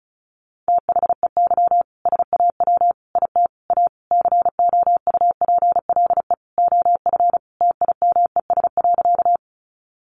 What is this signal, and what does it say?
Morse code, heard as a keyed tone, about 34 wpm, 700 Hz. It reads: THEY SAW IT A COUPLE OF TIMES.
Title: Sentences from Top 1100 Words 35wpm